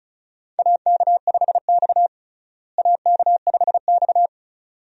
Morse code, heard as a keyed tone, about 36 wpm, 700 Hz. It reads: AK5X AK5X
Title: USA Call Signs - 4 Characters - Repeated 2x - 35wpm